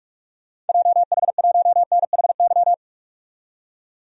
Morse code, heard as a keyed tone, about 46 wpm, 700 Hz. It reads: JH1NHY
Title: DXCC-100 Call Signs - 6 Characters - 45wpm